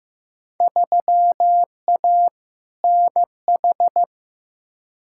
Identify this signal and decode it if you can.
Morse code, heard as a keyed tone, about 15 wpm, 700 Hz. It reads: 3A NH
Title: Field Day Exchanges - 15wpm